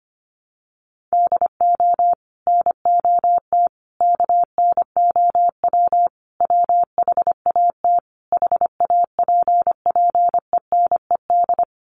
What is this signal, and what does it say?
Morse code, heard as a keyed tone, about 25 wpm, 700 Hz. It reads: DO NOT KNOW WHAT HAPPENED
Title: Sets of 5 Words Encore2 25wpm